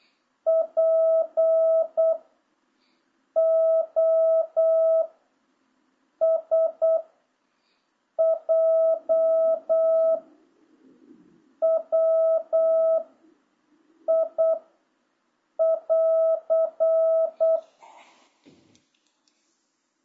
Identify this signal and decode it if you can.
Morse code, heard as a keyed tone, about 8 wpm, 650 Hz. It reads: P O S J W I <AR>